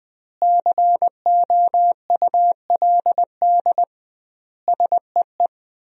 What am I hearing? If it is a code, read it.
Morse code, heard as a keyed tone, about 20 wpm, 700 Hz. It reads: COULD SEE
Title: 2x Word-Spacing - Sentences from Top 500 Words - 20wpm